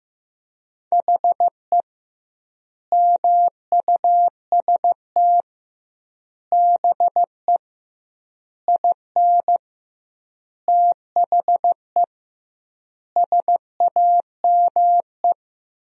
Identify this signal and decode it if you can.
Morse code, heard as a keyed tone, about 15 wpm, 700 Hz. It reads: HE MUST BE IN THE SAME
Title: 2x Word-Spacing - Sentences from Top 500 Words - 15wpm